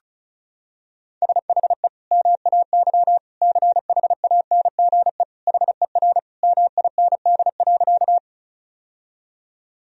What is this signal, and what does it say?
Morse code, heard as a keyed tone, about 36 wpm, 700 Hz. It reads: SHE MAY CHANGE HER MIND.
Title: Sentences from Top 1000 Words - Review Part 1 - 35wpm